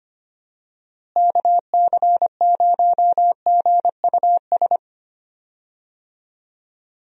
Morse code, heard as a keyed tone, about 25 wpm, 700 Hz. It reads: KC0GUS
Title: USA Call Signs - 6 Characters - Repeated 2x - 25wpm